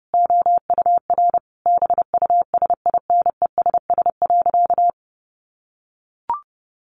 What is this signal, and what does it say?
Morse code, heard as a keyed tone, about 30 wpm, 700 Hz. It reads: OUR BUSINESS.
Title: Sentences from Top 1400 Words 30wpm